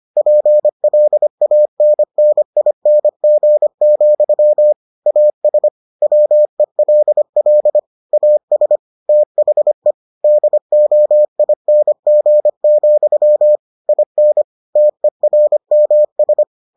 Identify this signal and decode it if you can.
Morse code, heard as a keyed tone, about 25 wpm, 600 Hz. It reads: PLANNING, AS WELL AS THE DOING, IN TERMS